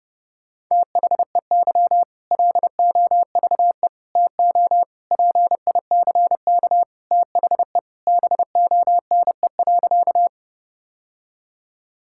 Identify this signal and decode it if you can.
Morse code, heard as a keyed tone, about 32 wpm, 700 Hz. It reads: THEY LOVE TO PICK THE BONE.
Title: Sentences from Top 1600 Words 30wpm